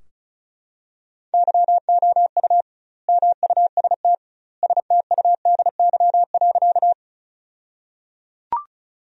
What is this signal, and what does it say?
Morse code, heard as a keyed tone, about 36 wpm, 700 Hz. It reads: YOU MUST STUDY.
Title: 2x Word-Spacing - Sentences from Top 300 Words - 35wpm